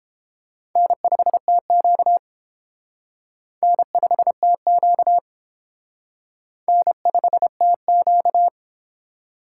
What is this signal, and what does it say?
Morse code, heard as a keyed tone, about 28 wpm, 700 Hz. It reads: N5TQ N5TQ N5TQ
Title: Speed Racing - USA 1x2 Call Signs 22wpm